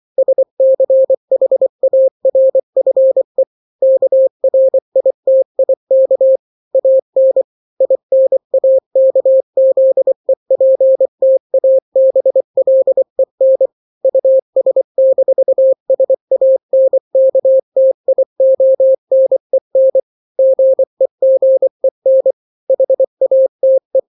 Morse code, heard as a keyed tone, about 24 wpm, 530 Hz. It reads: SCHARFE KRITIK AN INAKZEPTABLEN US-SANKTIONEN GEGEN HATE
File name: daily / 23wpm